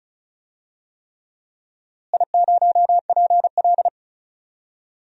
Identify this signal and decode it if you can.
Morse code, heard as a keyed tone, about 36 wpm, 700 Hz. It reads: I0PL
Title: DXCC-100 Call Signs - 4 Characters - Repeated 3x - 35wpm